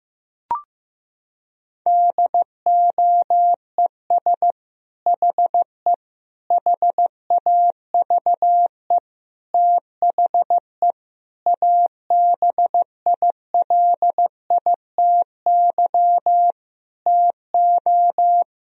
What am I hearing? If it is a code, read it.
Morse code, heard as a keyed tone, about 15 wpm, 700 Hz. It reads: DOES HE HAVE THE ABILITY TO
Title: Sentences from Top 1000 Words - Review Part 3 - 15wpm